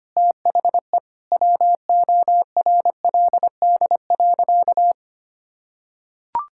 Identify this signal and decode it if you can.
Morse code, heard as a keyed tone, about 25 wpm, 700 Hz. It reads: THE WORLD.
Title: Sentences from Top 1600 Words 25wpm